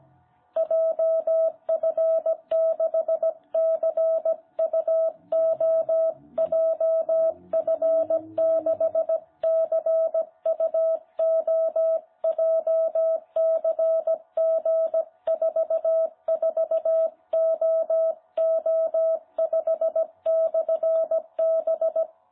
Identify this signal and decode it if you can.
Morse code, heard as a keyed tone, about 17 wpm, 660 Hz. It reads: JF6CUO JF6CUO JCG44OO5/B
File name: jh8cbh / 2021cw